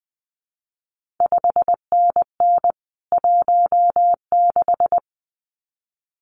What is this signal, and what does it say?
Morse code, heard as a keyed tone, about 20 wpm, 700 Hz. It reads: 5NN 16
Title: CQ WW DX Contest Exchange - 20wpm